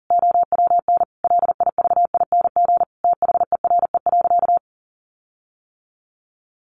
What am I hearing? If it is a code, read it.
Morse code, heard as a keyed tone, about 40 wpm, 700 Hz. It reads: OWN LIVING THERE.
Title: Sentences from Top 1500 Words 40wpm